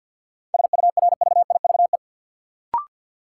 Morse code, heard as a keyed tone, about 53 wpm, 700 Hz. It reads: SURVIVE
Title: Sentences from Top 1300 Words - New Word Drill - 50wpm